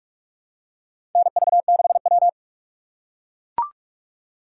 Morse code, heard as a keyed tone, about 43 wpm, 700 Hz. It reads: NV6W
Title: USA Call Signs - 4 Characters - 3x Character Spacing - 45wpm